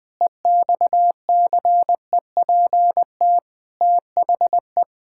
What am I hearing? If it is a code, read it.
Morse code, heard as a keyed tone, about 21 wpm, 700 Hz. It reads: EXCEPT THE
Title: Sentences from Top 1600 Words 20wpm